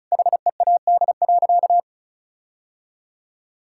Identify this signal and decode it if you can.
Morse code, heard as a keyed tone, about 34 wpm, 700 Hz. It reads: HEAD.
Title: Sentences from Top 1600 Words 35wpm